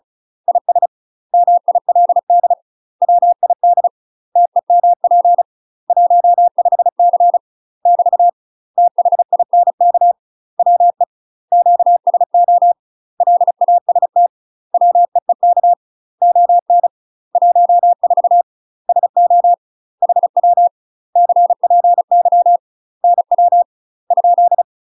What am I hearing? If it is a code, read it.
Morse code, heard as a keyed tone, about 36 wpm, 700 Hz. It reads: IS MILD WID TEMP 15C <BT> THINK WE QSO LAST WEEK ON 14 SO HW CPY NW ?
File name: UK MASTER QSOs TRAINING / 35WPM / UK MASTER QSOs 0026